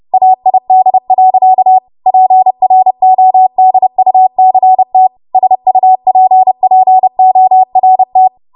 Morse code, heard as a keyed tone, about 30 wpm, 750 Hz. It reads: AID. PRODUCT SUPPORT
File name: Morse / 30WPM